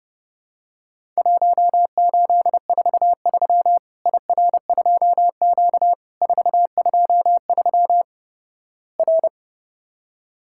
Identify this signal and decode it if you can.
Morse code, heard as a keyed tone, about 30 wpm, 700 Hz. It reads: 1843 IR2Q 423 R
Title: Worked All Europe DX Contest - QTC Exchange - 30wpm